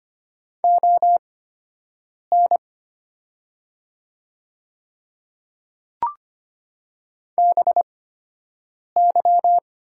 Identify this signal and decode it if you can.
Morse code, heard as a keyed tone, about 25 wpm, spaced as for 7 wpm, 700 Hz. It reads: ON BY
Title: 8x Character Spacing - Top 100 Words - 25wpm